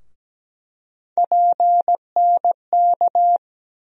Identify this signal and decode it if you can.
Morse code, heard as a keyed tone, about 17 wpm, 700 Hz. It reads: PNK
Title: USA Call Sign Postfixes - 3 Letters 17wpm